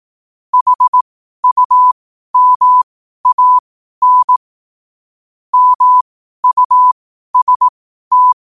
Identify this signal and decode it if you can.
Morse code, heard as a keyed tone, about 18 wpm, spaced as for 12 wpm, 1000 Hz. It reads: HUMAN MUST
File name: qotdfiles / 12wpm